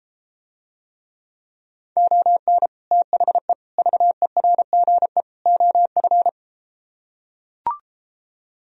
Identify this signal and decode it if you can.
Morse code, heard as a keyed tone, about 33 wpm, 700 Hz. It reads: ON THE VERGE OF